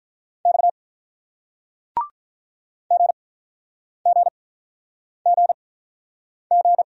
Morse code, heard as a keyed tone, about 47 wpm, spaced as for 9 wpm, 700 Hz. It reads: X GGGG